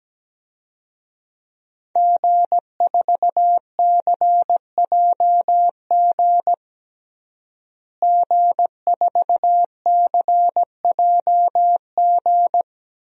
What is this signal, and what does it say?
Morse code, heard as a keyed tone, about 17 wpm, 700 Hz. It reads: G4CJG G4CJG
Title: DXCC-100 Call Signs - 5 Characters - Repeated 2x - 17wpm